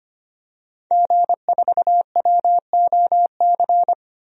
Morse code, heard as a keyed tone, about 25 wpm, 700 Hz. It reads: G4WOC
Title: DXCC-100 Call Signs - 5 Characters - 25wpm